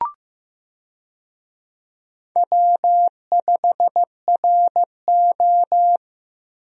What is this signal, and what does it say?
Morse code, heard as a keyed tone, about 15 wpm, 700 Hz. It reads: W5RO
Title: USA Call Signs - 4 Characters - Repeated 2x - 15wpm